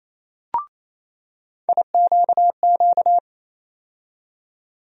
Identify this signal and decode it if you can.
Morse code, heard as a keyed tone, about 28 wpm, 700 Hz. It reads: IQQ
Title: USA Call Sign Postfixes - 3 Letters 28wpm